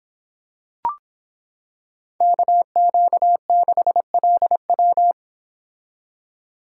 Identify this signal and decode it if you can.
Morse code, heard as a keyed tone, about 26 wpm, 700 Hz. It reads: KQ6LW